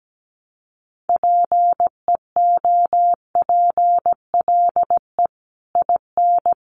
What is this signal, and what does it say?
Morse code, heard as a keyed tone, about 17 wpm, 700 Hz. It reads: PEOPLE IN
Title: Sets of 5 Words Encore 17wpm